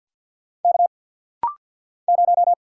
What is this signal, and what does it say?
Morse code, heard as a keyed tone, about 50 wpm, 700 Hz. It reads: K 0
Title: Single Letter-Number - Rapid-Fire 50wpm